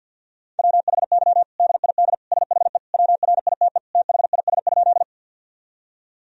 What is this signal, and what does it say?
Morse code, heard as a keyed tone, about 52 wpm, 700 Hz. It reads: WHY DID SHE WRITE THIS?
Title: Sentences from Top 1000 Words - Review Part 1 - 50wpm